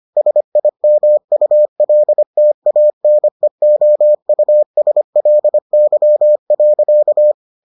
Morse code, heard as a keyed tone, about 25 wpm, 600 Hz. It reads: SIMULTANEOUSLY.